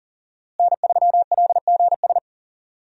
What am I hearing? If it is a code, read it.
Morse code, heard as a keyed tone, about 40 wpm, 700 Hz. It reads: N3LGS
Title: USA Call Signs - 5 Characters - Repeated 3x - 40wpm